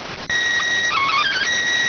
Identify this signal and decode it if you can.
SSTV (slow-scan television), PD 290.